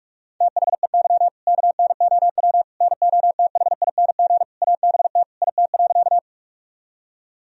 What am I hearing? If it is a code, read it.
Morse code, heard as a keyed tone, about 46 wpm, 700 Hz. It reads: THEY KNOW NOTHING ABT IT.
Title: Sentences from Top 500 Words and Abbreviations 45wpm